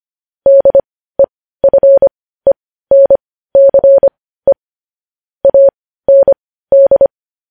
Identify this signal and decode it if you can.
Morse code, heard as a keyed tone, about 25 wpm, spaced as for 15 wpm, 550 Hz. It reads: DEFENCE AND
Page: news-headlines-in-morse-code-at-15-wpm